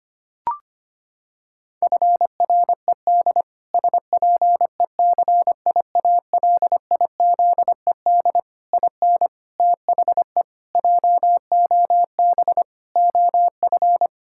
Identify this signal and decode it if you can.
Morse code, heard as a keyed tone, about 25 wpm, 700 Hz. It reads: FRED SPECIALIZED IN THE JOB OF
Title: Pangram Challenge v2 - 25wpm